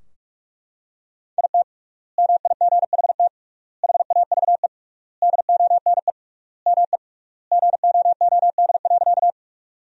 Morse code, heard as a keyed tone, about 42 wpm, 700 Hz. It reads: IT MIGHT HAVE DONE ME GOOD.
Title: 3x Word-Spacing - Sentences from Top 500 Words - 45wpm